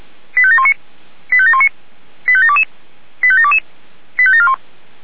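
Five-tone selective call (ZVEI), heard as quick runs of five tones, tones of about 80 ms. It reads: 85719 85719 85722 85722 85741
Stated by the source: ZVEI five-tone selective call 85727, received by SDR.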